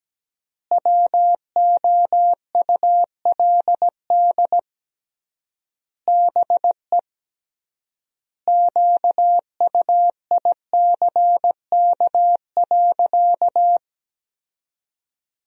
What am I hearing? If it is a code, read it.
Morse code, heard as a keyed tone, about 17 wpm, 700 Hz. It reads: WOULD BE QUICK.
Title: 3x Word-Spacing - Sentences from Top 500 Words - 17wpm